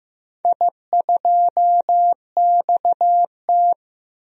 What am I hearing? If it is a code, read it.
Morse code, heard as a keyed tone, about 15 wpm, 700 Hz. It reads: I2XT
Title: DXCC-100 Call Signs - 4 Characters - Repeated 2x - 15wpm